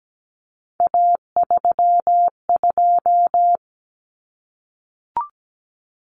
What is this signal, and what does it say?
Morse code, heard as a keyed tone, about 17 wpm, 700 Hz. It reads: A32
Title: Common Cut Numbers - 3-Digit 17wpm